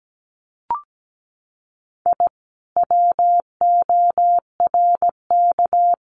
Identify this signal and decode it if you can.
Morse code, heard as a keyed tone, about 17 wpm, 700 Hz. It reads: I WORK
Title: Common QSO Phrases Encore 17wpm